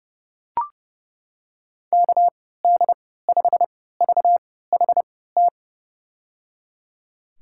Morse code, heard as a keyed tone, about 30 wpm, spaced as for 17 wpm, 700 Hz. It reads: KD5VHT
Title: USA Call Signs - 6 Characters - 3x Character Spacing - 30wpm